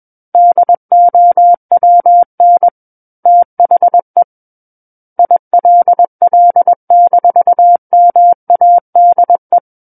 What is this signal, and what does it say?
Morse code, heard as a keyed tone, about 21 wpm, 700 Hz. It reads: DOWN THE ILL-MADE
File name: Book One / 12-Chapter 12 - 21 WPM